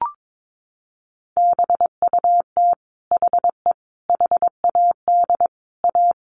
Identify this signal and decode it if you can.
Morse code, heard as a keyed tone, about 22 wpm, 700 Hz. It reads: BUT HE HAD A
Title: Sentences from Top 1200 Words 22wpm